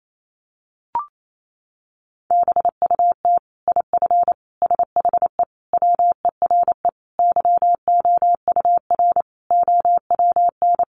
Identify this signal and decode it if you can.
Morse code, heard as a keyed tone, about 28 wpm, 700 Hz. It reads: BUT IF SHE WERE YOUR OWN
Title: Sentences from Top 1500 Words 28wpm